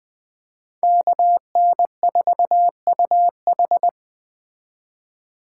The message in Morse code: KN4UH

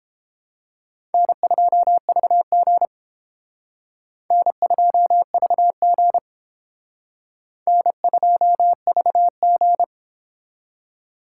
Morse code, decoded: N2VG N2VG N2VG

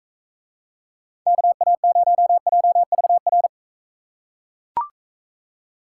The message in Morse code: KA0JVR

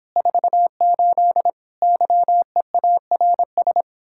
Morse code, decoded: 48 YEARS

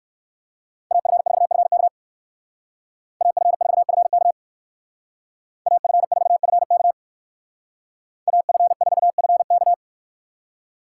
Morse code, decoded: AF4FK AF4FK AF4FK AF4FK

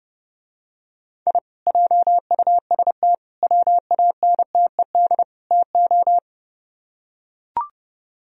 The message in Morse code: I JUST WANTED TO

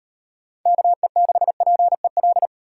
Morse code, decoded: KE6PEL

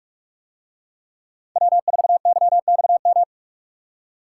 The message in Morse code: W4YXM

